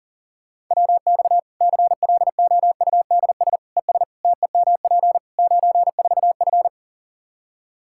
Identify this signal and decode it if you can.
Morse code, heard as a keyed tone, about 40 wpm, 700 Hz. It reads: WX CLOUDS ES TEMP 94F
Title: K1USN SST QSO Practice 40wpm